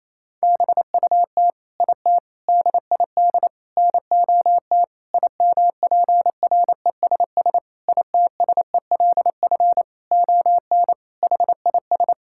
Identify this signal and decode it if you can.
Morse code, heard as a keyed tone, about 28 wpm, 700 Hz. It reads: BUT IT DID NOT IMPRESS ITSELF ON HIS
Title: Sentences from Top 1600 Words 28wpm